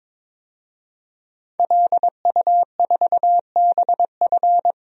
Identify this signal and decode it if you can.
Morse code, heard as a keyed tone, about 22 wpm, 700 Hz. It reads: LU4BF
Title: DXCC-100 Call Signs - 5 Characters - 22wpm